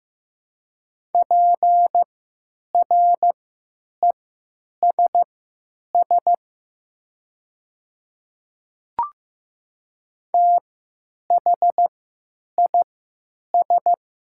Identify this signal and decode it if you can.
Morse code, heard as a keyed tone, about 15 wpm, spaced as for 9 wpm, 700 Hz. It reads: PRESS THIS
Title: 3x Character Spacing - Top 300 Words - 15wpm